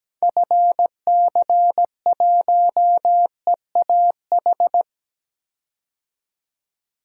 Morse code, heard as a keyed tone, about 17 wpm, 700 Hz. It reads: FC1EAH
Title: DXCC-100 Call Signs - 6 Characters - 17wpm